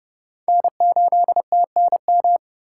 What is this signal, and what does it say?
Morse code, heard as a keyed tone, about 38 wpm, 700 Hz. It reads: N8TNM
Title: Speed Racing - USA 1x3 Call Signs 30wpm